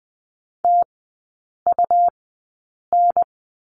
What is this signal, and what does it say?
Morse code, heard as a keyed tone, about 20 wpm, spaced as for 8 wpm, 690 Hz. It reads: TUN
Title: 159 Lesson 26e - Letter g - Three Characters 20wpm